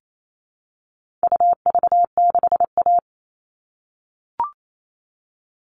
Morse code, heard as a keyed tone, about 28 wpm, 700 Hz. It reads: UV6A
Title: All Cut Numbers - 4-Digit 28wpm